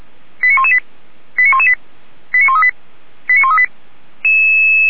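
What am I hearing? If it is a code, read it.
Five-tone selective call (ZVEI), heard as quick runs of five tones, tones of about 70 ms. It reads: 89118 89118 89138 89138
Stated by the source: ZVEI five-tone selective call 89107 received via SDR.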